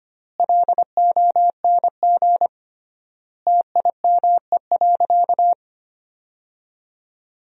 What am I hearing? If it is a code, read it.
Morse code, heard as a keyed tone, about 25 wpm, 700 Hz. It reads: LONG TIME.
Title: 3x Word-Spacing - Sentences from Top 100 Words - 25wpm